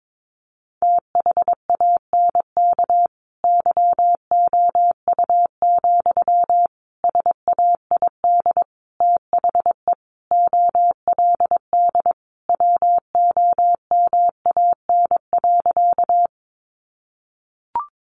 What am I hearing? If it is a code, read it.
Morse code, heard as a keyed tone, about 22 wpm, 700 Hz. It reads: THANK YOU, SAID THE OLD WOMAN.
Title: Sentences from Top 1100 Words 22wpm